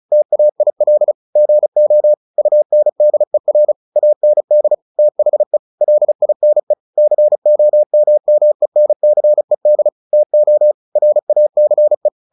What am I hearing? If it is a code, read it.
Morse code, heard as a keyed tone, about 34 wpm, 600 Hz. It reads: TAIL GO UNDER AND THE LINE COMMENCED TO RACE
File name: the old man and the sea / the old man and the sea 1 35wpm